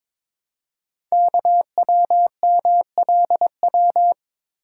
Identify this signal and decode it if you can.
Morse code, heard as a keyed tone, about 23 wpm, 700 Hz. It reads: KWMLW
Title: Code Groups - 5 Letters - 22wpm